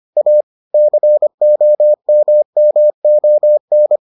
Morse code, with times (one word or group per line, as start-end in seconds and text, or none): 0.2-0.4 A
0.7-4.0 COMMON